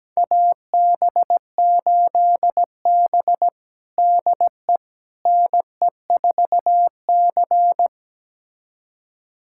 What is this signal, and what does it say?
Morse code, heard as a keyed tone, about 17 wpm, 700 Hz. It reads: AB8B DE NE4C
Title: USA Callsign DE Callsign - 4 Characters - 17wpm